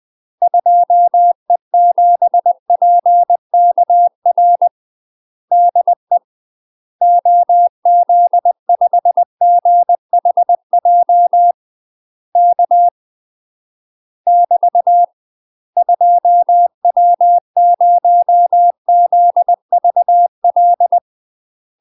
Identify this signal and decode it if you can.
Morse code, heard as a keyed tone, about 20 wpm, 700 Hz. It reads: 2E7PKR DE OZ5GHJ K <BT> 2W0ZVL
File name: UK MASTER QSOs TRAINING / 20WPM / UK MASTER QSOs 0045